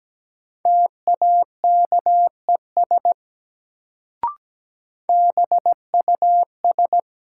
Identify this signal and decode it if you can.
Morse code, heard as a keyed tone, about 17 wpm, 700 Hz. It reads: TAKES BUS
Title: Sentences from Top 1000 Words - Review Part 6 - New Word Drill - 17wpm